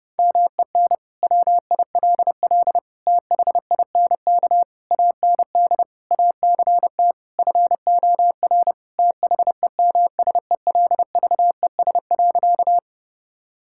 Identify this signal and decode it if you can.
Morse code, heard as a keyed tone, about 30 wpm, 700 Hz. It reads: MEN WILL THINK AND ACT FOR THEMSELVES.
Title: Sentences from Top 1000 Words - Review Part 5 - 30wpm